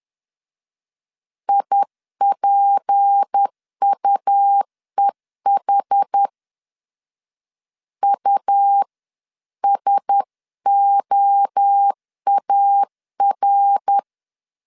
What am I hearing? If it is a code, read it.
Morse code, heard as a keyed tone, about 10 wpm, 800 Hz. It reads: IPUEH USOAR
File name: LEZIONI / 7 GF9Z / 7 GF 10wpm 2514